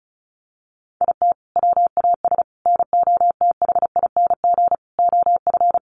Morse code, heard as a keyed tone, about 34 wpm, 700 Hz. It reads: IT WAS NOTHING OF
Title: Sentences from Top 1300 Words 35wpm